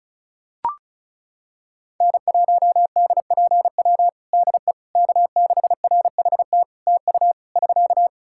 Morse code, heard as a keyed tone, about 36 wpm, 700 Hz. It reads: N1DPW DE K6RHT TU <SK>
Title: Practice QSO - Exchange 4 - Ending - 35wpm